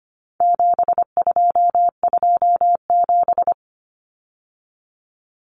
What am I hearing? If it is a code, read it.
Morse code, heard as a keyed tone, about 25 wpm, 700 Hz. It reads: 7227
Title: Common Cut Numbers - 4-Digit 25wpm